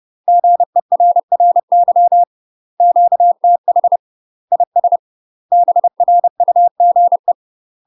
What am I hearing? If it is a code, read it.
Morse code, heard as a keyed tone, about 30 wpm, 700 Hz. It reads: GERRY QTH IS BRUGE
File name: UK MASTER QSOs TRAINING / 30WPM / UK MASTER QSOs 0044